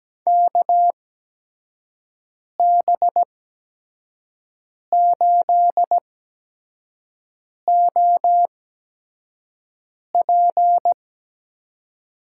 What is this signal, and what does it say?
Morse code, heard as a keyed tone, about 17 wpm, spaced as for 5 wpm, 700 Hz. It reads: KB8OP
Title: USA Call Signs - 8x Character Spacing - 17wpm